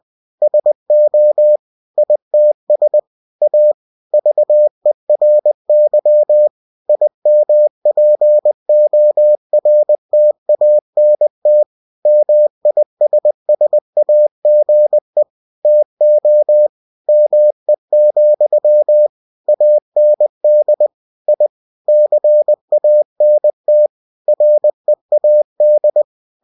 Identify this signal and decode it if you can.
Morse code, heard as a keyed tone, about 20 wpm, 600 Hz. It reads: SO ITS A VERY IMPORTANT MISSAGE TO ME, AND I CANT READ